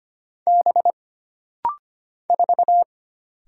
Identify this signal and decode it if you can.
Morse code, heard as a keyed tone, about 25 wpm, 700 Hz. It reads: B 4
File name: Single Character (Letter:Number:Punctuation:Prosign) - Rapid-Fire - 25wpm